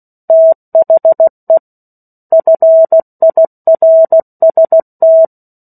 Morse code, heard as a keyed tone, about 16 wpm, 650 Hz. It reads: THE FIRST
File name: 03-Chapter-03 16 WPM